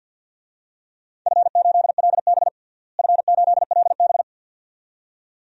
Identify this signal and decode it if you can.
Morse code, heard as a keyed tone, about 47 wpm, 700 Hz. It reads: F8LB F8LB
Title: DXCC-100 Call Signs - 4 Characters - Repeated 2x - 50wpm